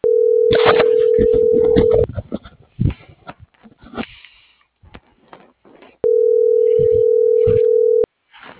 Ringback tone (North America), 440 + 480 Hz.